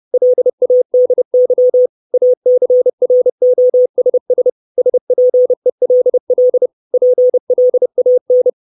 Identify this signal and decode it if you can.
Morse code, heard as a keyed tone, about 30 wpm, 500 Hz. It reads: LADY ACROSS SPELL PLAN
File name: 632 Words 30WPM